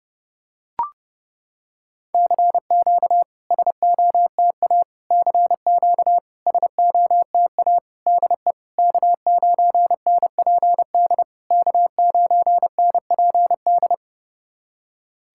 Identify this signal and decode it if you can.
Morse code, heard as a keyed tone, about 30 wpm, 690 Hz. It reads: CQ SOTA CQ SOTA DE K9NPD K9NPD
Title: SOTA QSO Practice 30wpm